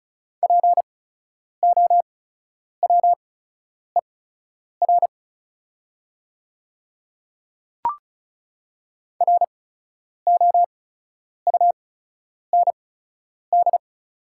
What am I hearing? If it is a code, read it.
Morse code, heard as a keyed tone, about 36 wpm, spaced as for 10 wpm, 700 Hz. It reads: POWER ROUND
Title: 8x Character Spacing - Top 500 Words - 35wpm